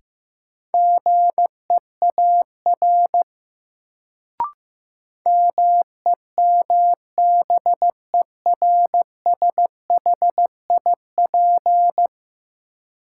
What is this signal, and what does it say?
Morse code, heard as a keyed tone, about 15 wpm, 700 Hz. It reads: GEAR MEMBERSHIP
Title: Sentences from Top 1400 Words - New Word Drill Alt - 15wpm